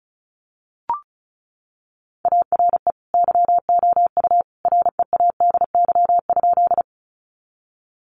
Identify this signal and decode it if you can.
Morse code, heard as a keyed tone, about 34 wpm, 700 Hz. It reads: ARE YOU READY?
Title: Sentences from Top 400 Words 35wpm